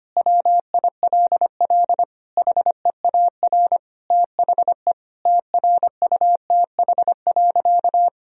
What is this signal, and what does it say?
Morse code, heard as a keyed tone, about 25 wpm, 700 Hz. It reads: WILL HEAR THE TRUTH.
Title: Sentences from Top 1000 Words - Review Part 1 - 25wpm